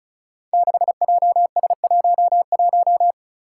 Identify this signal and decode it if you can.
Morse code, heard as a keyed tone, about 36 wpm, 700 Hz. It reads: 6JS11
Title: Five Challenging Letter-Number Combinations 35wpm